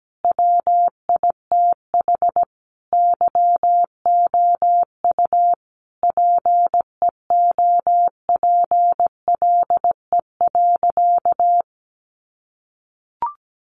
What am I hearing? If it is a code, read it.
Morse code, heard as a keyed tone, about 17 wpm, 700 Hz. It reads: WITH YOU PEOPLE.
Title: Sentences from Top 300 Words 17wpm